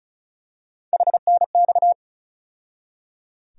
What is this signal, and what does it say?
Morse code, heard as a keyed tone, about 36 wpm, 700 Hz. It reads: HNX
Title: USA Call Sign Postfixes - 3 Letters 35wpm